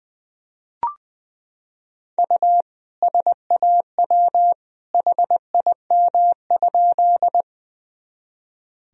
Morse code, heard as a keyed tone, about 20 wpm, 700 Hz. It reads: U SAW HIM?
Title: Sentences from Top 300 Words and Abbreviations 20wpm